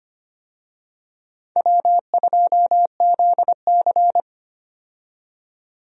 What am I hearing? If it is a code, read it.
Morse code, heard as a keyed tone, about 25 wpm, 700 Hz. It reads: W2ZC
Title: USA Call Signs - 4 Characters - 25wpm